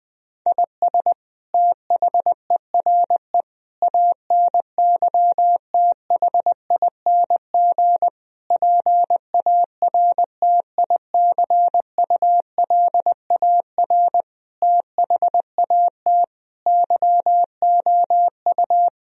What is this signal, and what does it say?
Morse code, heard as a keyed tone, about 20 wpm, 700 Hz. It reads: IS THERE ANYTHING PARTICULAR THAT YOU
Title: Sentences from Top 1000 Words - Review Part 4 - 20wpm